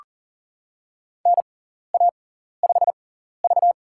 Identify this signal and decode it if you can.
Morse code, heard as a keyed tone, about 40 wpm, spaced as for 14 wpm, 700 Hz. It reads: NA5V
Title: USA Call Signs - 6x Character Spacing - 40wpm